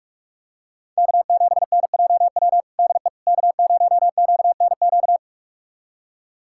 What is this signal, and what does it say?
Morse code, heard as a keyed tone, about 46 wpm, 700 Hz. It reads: K7NJW DE K0QNQ
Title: USA Callsign DE Callsign - 5 Characters - 45wpm